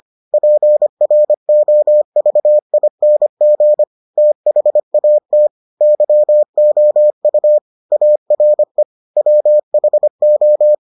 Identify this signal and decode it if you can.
Morse code, heard as a keyed tone, about 25 wpm, 600 Hz. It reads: PROVING THAT YOU ARE WHO